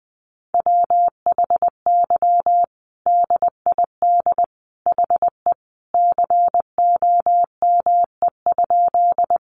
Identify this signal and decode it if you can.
Morse code, heard as a keyed tone, about 20 wpm, 700 Hz. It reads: WHY DID HE COME?
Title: Sentences from Top 200 Words 20wpm